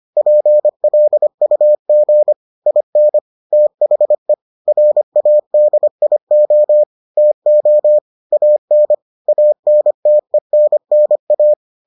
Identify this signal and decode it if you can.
Morse code, heard as a keyed tone, about 25 wpm, 600 Hz. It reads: PLUG IN THE RADIO TO AN ANTENNA